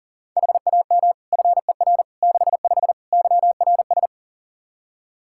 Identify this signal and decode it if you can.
Morse code, heard as a keyed tone, about 40 wpm, 700 Hz. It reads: HAM FER 65 YRS
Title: Practice QSO - Exchange 3 - More Info - 40wpm